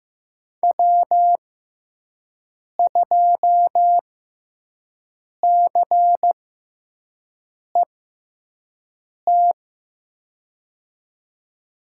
Morse code, heard as a keyed tone, about 15 wpm, spaced as for 5 wpm, 700 Hz. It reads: W2CET
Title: USA Call Signs - 5 Characters - 6x Character Spacing - 15wpm